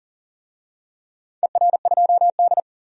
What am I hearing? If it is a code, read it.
Morse code, heard as a keyed tone, about 38 wpm, 700 Hz. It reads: ER2D
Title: DXCC-100 Call Signs - 4 Characters - 40wpm